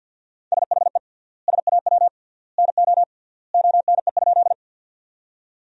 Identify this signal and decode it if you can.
Morse code, heard as a keyed tone, about 46 wpm, 700 Hz. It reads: SHE SAW NO ONE?
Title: 3x Word-Spacing - Sentences from Top 300 Words - 50wpm